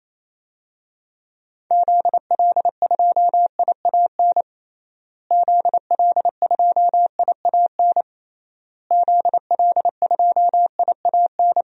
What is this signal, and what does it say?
Morse code, heard as a keyed tone, about 28 wpm, 700 Hz. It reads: ZL2IAN ZL2IAN ZL2IAN
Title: DXCC-100 Call Signs - 6 Characters - Repeated 3x - 28wpm